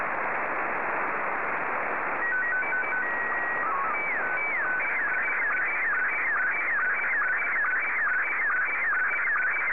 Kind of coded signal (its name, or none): SSTV (slow-scan television)